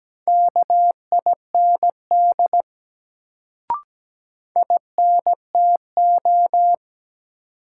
Morse code, heard as a keyed tone, about 17 wpm, 700 Hz. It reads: KIND INTO
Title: Sets of 2 Words - New Word Drill - 17wpm